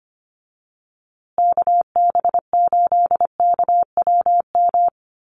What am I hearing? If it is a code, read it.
Morse code, heard as a keyed tone, about 25 wpm, 700 Hz. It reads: KB8KWM
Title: USA Call Signs Repeated 3x 25wpm